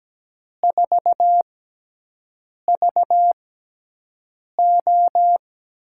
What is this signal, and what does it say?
Morse code, heard as a keyed tone, about 17 wpm, spaced as for 6 wpm, 700 Hz. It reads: 4VO
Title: USA Call Signs - 6 Characters - 6x Character Spacing - 17wpm